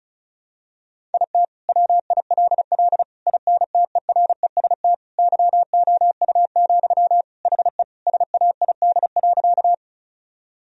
Morse code, heard as a keyed tone, about 36 wpm, 700 Hz. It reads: IT WILL INTEREST YOU, HE SAID.
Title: Sentences from Top 1000 Words - Review Part 2 - 35wpm